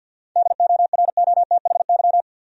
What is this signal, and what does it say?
Morse code, heard as a keyed tone, about 52 wpm, 700 Hz. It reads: DOROTHY
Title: USA Common Names - 50wpm